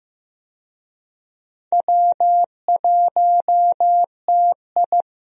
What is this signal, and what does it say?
Morse code, heard as a keyed tone, about 16 wpm, 700 Hz. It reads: W1TI
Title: USA Call Signs - 4 Characters - Repeated 2x - 15wpm